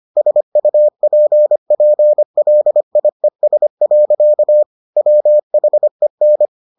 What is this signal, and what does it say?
Morse code, heard as a keyed tone, about 25 wpm, 600 Hz. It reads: SUPPLIES. WHEN